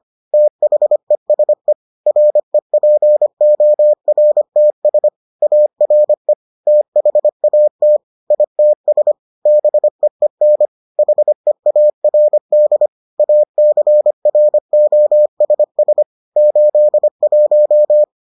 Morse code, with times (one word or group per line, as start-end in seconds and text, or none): 0.3-1.7 THESE
2.1-5.1 REPORTS
5.4-6.3 ARE
6.7-8.0 THAT
8.3-9.1 ITS
9.5-10.7 BEEN
11.0-12.9 HEARD
13.2-16.0 ACROSS
16.4-18.0 81